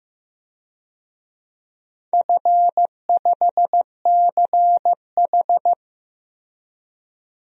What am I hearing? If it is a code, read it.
Morse code, heard as a keyed tone, about 15 wpm, 700 Hz. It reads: F5CH
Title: DXCC-100 Call Signs - 4 Characters - Repeated 3x - 15wpm